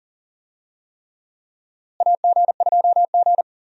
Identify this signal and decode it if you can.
Morse code, heard as a keyed tone, about 40 wpm, 700 Hz. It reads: AG2G
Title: USA Call Signs - 4 Characters - Repeated 3x - 40wpm